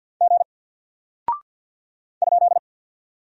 Morse code, heard as a keyed tone, about 48 wpm, 700 Hz. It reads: G ?
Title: Single Character (Letter:Number:Punctuation:Prosign) Alt 50wpm